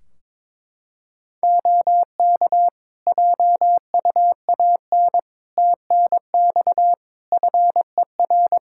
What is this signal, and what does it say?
Morse code, heard as a keyed tone, about 22 wpm, 700 Hz. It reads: OK JUAN TNX FER
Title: Practice QSO - Exchange 4 - Ending - 22wpm